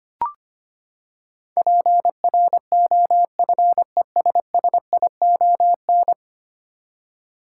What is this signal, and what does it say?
Morse code, heard as a keyed tone, about 25 wpm, 700 Hz. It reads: PROFESSION
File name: Ten Letter Words - 25wpm